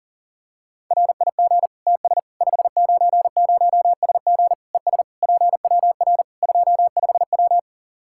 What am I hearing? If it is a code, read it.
Morse code, heard as a keyed tone, about 40 wpm, 700 Hz. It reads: RIG TS 590SG ES PWR 25W
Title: Practice QSO - Exchange 2 - Info - 40wpm